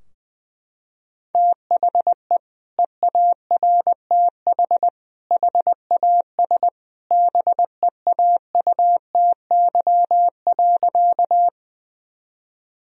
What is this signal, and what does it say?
Morse code, heard as a keyed tone, about 20 wpm, 700 Hz. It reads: THE EARTH HAS BEAUTY.
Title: Sentences from Top 1000 Words - Review Part 2 - 20wpm